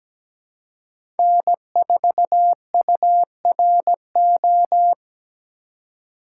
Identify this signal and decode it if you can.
Morse code, heard as a keyed tone, about 17 wpm, 700 Hz. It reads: N4URO
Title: USA Call Signs - 5 Characters - 2x Character Spacing - 17wpm